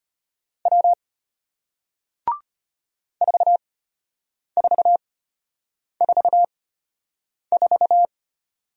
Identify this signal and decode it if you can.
Morse code, heard as a keyed tone, about 33 wpm, spaced as for 8 wpm, 700 Hz. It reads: W 4444